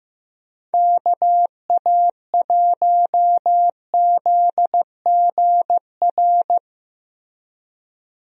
Morse code, heard as a keyed tone, about 15 wpm, 700 Hz. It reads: KA1ZGR
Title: USA Call Signs - 6 Characters - Repeated 3x - 15wpm